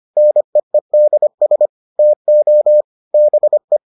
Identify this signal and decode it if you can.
Morse code, heard as a keyed tone, about 25 wpm, 600 Hz. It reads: NEEDS TO BE